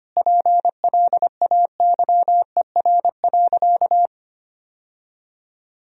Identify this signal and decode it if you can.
Morse code, heard as a keyed tone, about 25 wpm, 700 Hz. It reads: PLAYER.
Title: Sentences from Top 1000 Words - Review Part 4 - 25wpm